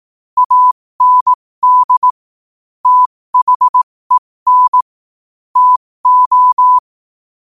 Morse code, heard as a keyed tone, about 18 wpm, spaced as for 15 wpm, 1000 Hz. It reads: AND THEN TO
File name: qotdfiles / 15wpm